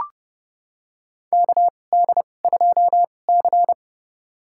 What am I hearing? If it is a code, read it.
Morse code, heard as a keyed tone, about 30 wpm, spaced as for 22 wpm, 700 Hz. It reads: KD2C
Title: USA Call Signs - 2x Character Spacing - 30wpm